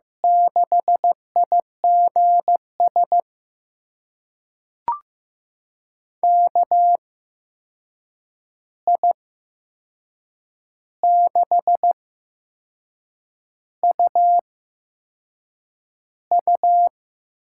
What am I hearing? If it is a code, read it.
Morse code, heard as a keyed tone, about 15 wpm, spaced as for 6 wpm, 700 Hz. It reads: 6IGS KI6UU
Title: USA Call Signs - 6 Characters - 8x Character Spacing - 15wpm